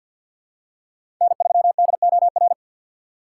Morse code, heard as a keyed tone, about 48 wpm, 700 Hz. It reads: N3DOR
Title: USA Call Signs - 5 Characters - 4x Character Spacing - 50wpm